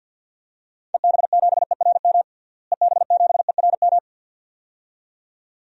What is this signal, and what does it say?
Morse code, heard as a keyed tone, about 50 wpm, 700 Hz. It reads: EB7ERM EB7ERM
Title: DXCC-100 Call Signs - 6 Characters - Repeated 2x - 50wpm